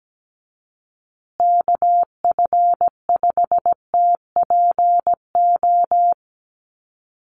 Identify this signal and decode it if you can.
Morse code, heard as a keyed tone, about 17 wpm, 700 Hz. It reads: KF5TPO
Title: USA Call Signs Repeated 2x 17wpm